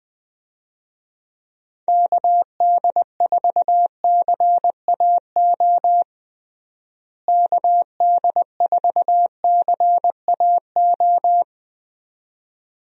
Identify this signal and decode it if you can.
Morse code, heard as a keyed tone, about 20 wpm, 700 Hz. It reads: KD4CAO KD4CAO
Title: USA Call Signs - 6 Characters - Repeated 2x - 20wpm